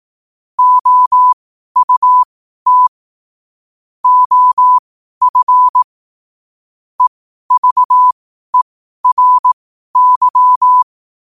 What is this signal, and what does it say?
Morse code, heard as a keyed tone, about 18 wpm, spaced as for 12 wpm, 1000 Hz. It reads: OUT OF EVERY
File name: qotdfiles / 12wpm